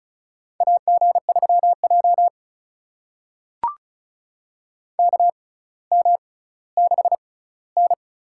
Morse code, heard as a keyed tone, about 36 wpm, spaced as for 18 wpm, 700 Hz. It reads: AG3J KM6N
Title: USA Call Signs - 4 Characters - 6x Character Spacing - 35wpm